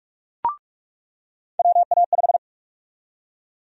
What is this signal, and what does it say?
Morse code, heard as a keyed tone, about 46 wpm, 700 Hz. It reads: WA5
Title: USA Call Sign Prefixes - 45wpm